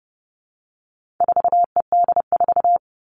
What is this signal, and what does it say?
Morse code, heard as a keyed tone, about 30 wpm, 700 Hz. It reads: 4ED4
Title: All Cut Numbers - 4-Digit 30wpm